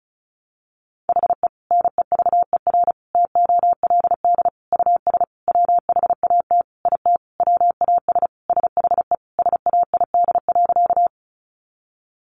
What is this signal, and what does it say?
Morse code, heard as a keyed tone, about 34 wpm, 700 Hz. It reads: HE NEVER TOLD US WHAT IT WAS SHE SAID.
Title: Sentences from Top 400 Words 35wpm